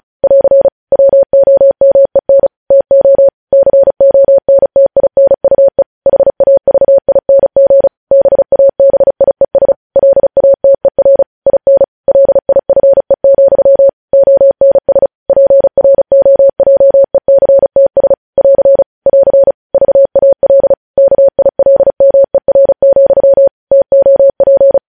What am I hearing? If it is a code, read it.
Morse code, heard as a keyed tone, about 36 wpm, 550 Hz. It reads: <AR> WOMEN TO CONTINUE HAVING BABIES LATER IN LIFE, ONS PROJECTS <AR> <AR> VAL KILMER, TOP
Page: news headlines in morse code at 35 wpm